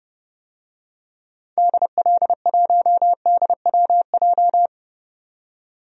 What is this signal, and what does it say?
Morse code, heard as a keyed tone, about 32 wpm, 700 Hz. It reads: DL1DWJ
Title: DXCC-100 Call Signs - 6 Characters - 30wpm